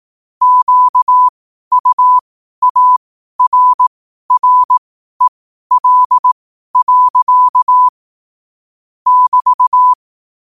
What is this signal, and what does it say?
Morse code, heard as a keyed tone, about 18 wpm, spaced as for 12 wpm, 1000 Hz. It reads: QUARREL. <BT>
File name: qotdfiles / 12wpm